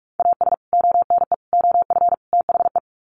Morse code, heard as a keyed tone, about 44 wpm, 700 Hz. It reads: AS ONE OF THE
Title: Sets of 4 Words 45wpm